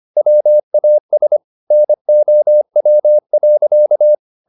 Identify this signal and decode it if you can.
Morse code, heard as a keyed tone, about 25 wpm, 600 Hz. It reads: WAS NOW.
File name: the old man and the sea part 1 25wpm